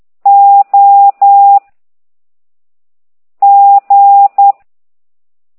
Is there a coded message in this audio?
Yes: Morse code from 0.3 to 4.5 s, heard as a keyed tone, about 10 wpm, 800 Hz. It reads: O G